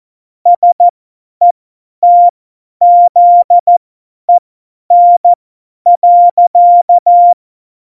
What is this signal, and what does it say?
Morse code, heard as a keyed tone, about 14 wpm, spaced as for 10 wpm, 700 Hz. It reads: SETZEN.
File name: Morsekurs / Witze / Witz 11 August 2020, 50 F70